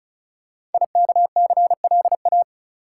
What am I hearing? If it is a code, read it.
Morse code, heard as a keyed tone, about 36 wpm, 700 Hz. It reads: IKCLA
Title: Code Groups - 5 Letter:Numbers - 35wpm